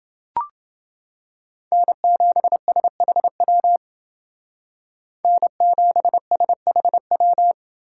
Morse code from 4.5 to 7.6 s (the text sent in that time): N7SHW